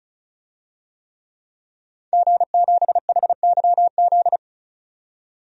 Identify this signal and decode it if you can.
Morse code, heard as a keyed tone, about 34 wpm, 700 Hz. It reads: G7HYZ
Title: DXCC-100 Call Signs - 5 Characters - 35wpm